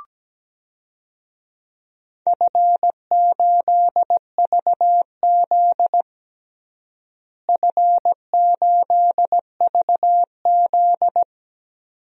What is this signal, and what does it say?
Morse code, heard as a keyed tone, about 17 wpm, 700 Hz. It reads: F8VZ F8VZ
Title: DXCC-100 Call Signs - 4 Characters - Repeated 2x - 17wpm